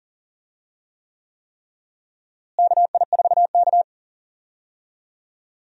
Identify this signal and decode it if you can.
Morse code, heard as a keyed tone, about 40 wpm, 700 Hz. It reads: KI4K
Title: USA Call Signs - 4 Characters - Repeated 3x - 40wpm